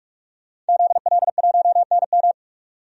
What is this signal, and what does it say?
Morse code, heard as a keyed tone, about 42 wpm, 700 Hz. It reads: ZL1NM
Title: DXCC-100 Call Signs - 5 Characters - 45wpm